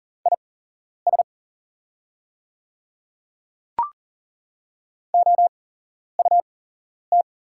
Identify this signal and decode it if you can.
Morse code, heard as a keyed tone, about 40 wpm, spaced as for 11 wpm, 690 Hz. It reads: IS OUT
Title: 8x Character Spacing - Top 200 Words - 40wpm